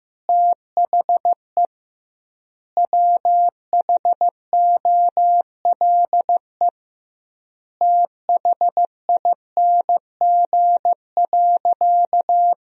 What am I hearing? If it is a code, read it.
Morse code, heard as a keyed tone, about 15 wpm, 700 Hz. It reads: THE WHOLE THING.
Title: 2x Word-Spacing - Sentences from Top 400 Words - 15wpm